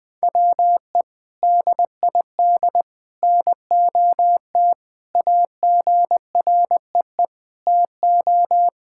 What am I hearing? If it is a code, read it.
Morse code, heard as a keyed tone, about 20 wpm, 700 Hz. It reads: WE DID NOT AGREE TO
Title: Sentences with 3-Word Phrases - 20wpm